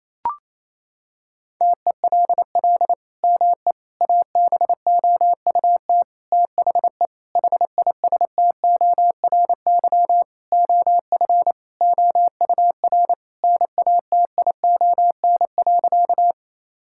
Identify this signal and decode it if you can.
Morse code, heard as a keyed tone, about 28 wpm, 700 Hz. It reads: TELL ME ABOUT THE HISTORY OF OUR NATION.
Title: Sentences from Top 1000 Words - Review Part 6 - 28wpm